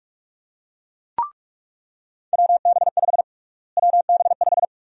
Morse code, heard as a keyed tone, about 42 wpm, 700 Hz. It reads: WB5 WB5
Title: USA Call Sign Prefixes - Repeated 2x - 45wpm